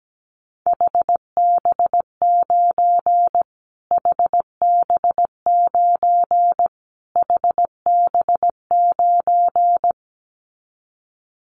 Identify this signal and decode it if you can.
Morse code, heard as a keyed tone, about 17 wpm, 700 Hz. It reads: HB9 HB9 HB9
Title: Callsign Prefixes for Top 25 DXCC - Repeated 3x - 17wpm